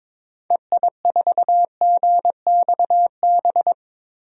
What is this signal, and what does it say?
Morse code, heard as a keyed tone, about 22 wpm, 700 Hz. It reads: EI4GXB
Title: DXCC-100 Call Signs - 6 Characters - 22wpm